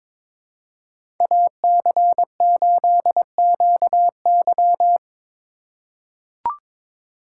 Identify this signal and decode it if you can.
Morse code, heard as a keyed tone, about 22 wpm, 700 Hz. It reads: AC8QY
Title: USA Call Signs - 3x Character Spacing - 22wpm